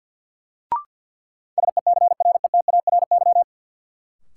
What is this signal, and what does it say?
Morse code, heard as a keyed tone, about 52 wpm, 700 Hz. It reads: SECRETARY
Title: Sentences from Top 1300 Words - New Word Drill - 50wpm